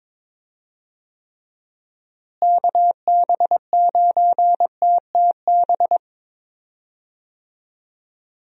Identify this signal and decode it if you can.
Morse code, heard as a keyed tone, about 22 wpm, 700 Hz. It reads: KB9TTB
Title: USA Call Signs - 6 Characters - Repeated 2x - 22wpm